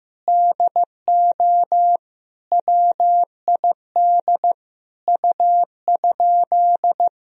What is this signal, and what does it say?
Morse code, heard as a keyed tone, about 15 wpm, 700 Hz. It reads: DO WID U?
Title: Sentences from Top 100 Words and Abbreviations 15wpm